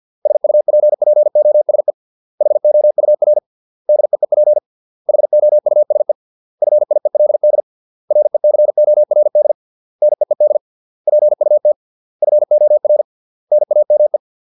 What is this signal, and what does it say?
Morse code, heard as a keyed tone, about 50 wpm, 600 Hz. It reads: SUPPOSE HOUR DEEP HOUSE FIELD RECORD NEED PUT FOR NAME